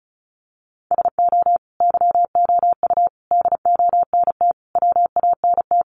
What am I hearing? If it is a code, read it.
Morse code, heard as a keyed tone, about 34 wpm, 700 Hz. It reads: SO YOU DONT WANT
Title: Sentences from Top 1000 Words 35wpm